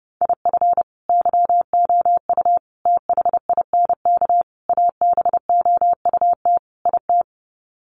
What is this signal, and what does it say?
Morse code, heard as a keyed tone, about 28 wpm, 700 Hz. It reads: IF YOU THINK ABOUT IT
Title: Sets of 5 Words Encore2 30wpm